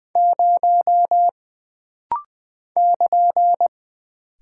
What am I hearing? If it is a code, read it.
Morse code, heard as a keyed tone, about 20 wpm, 700 Hz. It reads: 0 <KN>
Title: Single Character (Letter:Number:Punctuation:Prosign) - Rapid-Fire - 20wpm